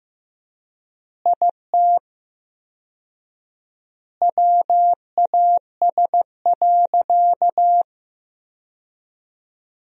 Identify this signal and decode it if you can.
Morse code, heard as a keyed tone, about 15 wpm, 700 Hz. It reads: IT WAS.
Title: 4x Word-Spacing - Sentences from Top 100 Words - 15wpm